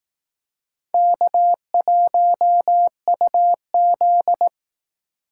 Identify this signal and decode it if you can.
Morse code, heard as a keyed tone, about 18 wpm, 700 Hz. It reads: K1UZ
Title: Speed Racing - USA 1x2 Call Signs 15wpm